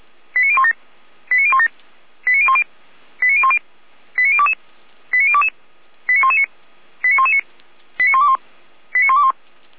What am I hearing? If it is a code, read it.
Five-tone selective call (ZVEI), heard as quick runs of five tones, tones of about 70 ms. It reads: 89017 89017 89010 89010 89022 89022 89119 89119 89121 89121